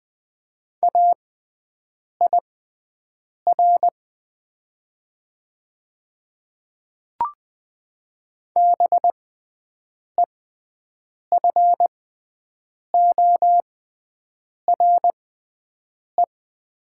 Morse code, heard as a keyed tone, about 20 wpm, spaced as for 7 wpm, 700 Hz. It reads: AIR BEFORE
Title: 6x Character Spacing - Top 200 Words - 20wpm